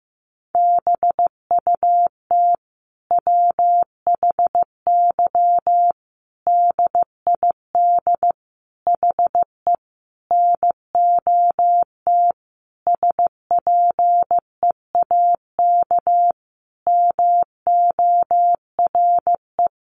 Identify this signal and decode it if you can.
Morse code, heard as a keyed tone, about 15 wpm, 700 Hz. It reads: BUT WHY DID HE NOT SPEAK MORE
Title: Sentences from Top 1300 Words 15wpm